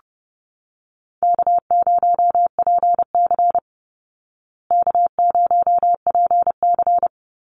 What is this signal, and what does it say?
Morse code, heard as a keyed tone, about 30 wpm, 700 Hz. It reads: K0PC K0PC
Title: Easy Call Signs Repeated 2x - 30wpm